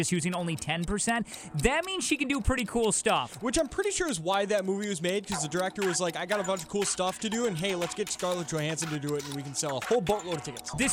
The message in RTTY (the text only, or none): none